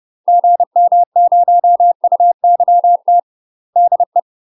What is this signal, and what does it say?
Morse code, heard as a keyed tone, about 30 wpm, 700 Hz. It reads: GM0UYT DE
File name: UK MASTER QSOs TRAINING / 30WPM / UK MASTER QSOs 0004